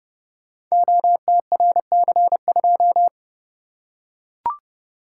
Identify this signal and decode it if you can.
Morse code, heard as a keyed tone, about 30 wpm, 700 Hz. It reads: OTRC2
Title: Code Groups - 5 Letter:Numbers - 30wpm